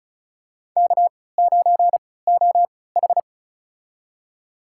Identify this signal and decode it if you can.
Morse code, heard as a keyed tone, about 34 wpm, spaced as for 20 wpm, 700 Hz. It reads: K9OH
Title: USA Call Signs - 3x Character Spacing - 35wpm